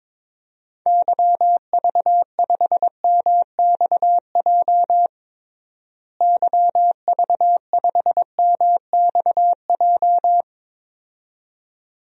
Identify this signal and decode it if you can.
Morse code, heard as a keyed tone, about 22 wpm, 700 Hz. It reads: YV5MXJ YV5MXJ
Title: DXCC-100 Call Signs - 6 Characters - Repeated 2x - 22wpm